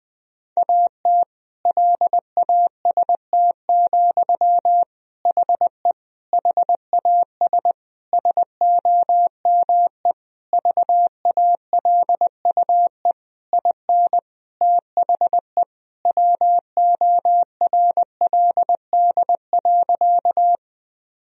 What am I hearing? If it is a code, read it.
Morse code, heard as a keyed tone, about 21 wpm, 700 Hz. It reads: AT LAST, HE HAS SOME VALUE IN THE WORLD.
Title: Sentences from Top 1000 Words - Review Part 3 - 20wpm